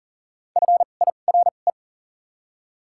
Morse code, heard as a keyed tone, about 38 wpm, 700 Hz. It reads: F I R E
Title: 2x Character Spacing - Top 500 Words - 40wpm